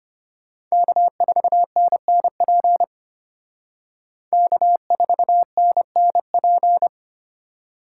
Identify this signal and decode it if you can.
Morse code, heard as a keyed tone, about 32 wpm, 700 Hz. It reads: K4NNP K4NNP